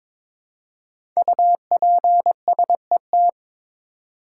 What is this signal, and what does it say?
Morse code, heard as a keyed tone, about 22 wpm, 690 Hz. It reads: UPSET
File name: 5 Letter Words - First 3 Letters Given - 22wpm